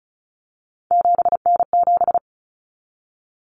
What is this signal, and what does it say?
Morse code, heard as a keyed tone, about 34 wpm, 700 Hz. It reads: 7N7
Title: Common Cut Numbers - 3-Digit 35wpm